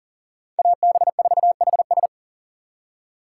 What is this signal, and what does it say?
Morse code, heard as a keyed tone, about 52 wpm, 700 Hz. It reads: AB4HS